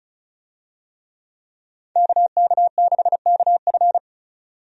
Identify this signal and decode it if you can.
Morse code, heard as a keyed tone, about 36 wpm, 700 Hz. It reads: KK6KF
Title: USA Call Signs - 5 Characters - 35wpm